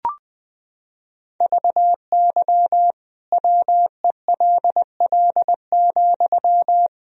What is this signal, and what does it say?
Morse code, heard as a keyed tone, about 21 wpm, 700 Hz. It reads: VY WELL,
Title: Sentences from Top 200 Words and Abbreviations 20wpm